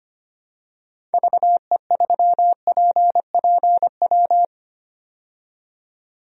Morse code, heard as a keyed tone, about 25 wpm, 700 Hz. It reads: VE3PPW
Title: DXCC-100 Call Signs - 6 Characters - 25wpm